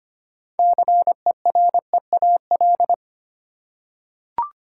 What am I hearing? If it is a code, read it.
Morse code, heard as a keyed tone, about 25 wpm, 700 Hz. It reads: CEREAL
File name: practice / 25wpm / Easy Words with Hints - 25wpm